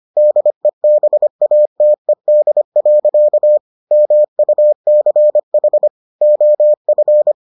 Morse code, heard as a keyed tone, about 25 wpm, 600 Hz. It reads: DEBATED. MUCH OF